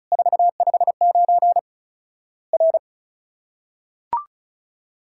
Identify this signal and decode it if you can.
Morse code, heard as a keyed tone, about 36 wpm, 700 Hz. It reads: 459 R
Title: Worked All Europe DX Contest - QTC Exchange - 35wpm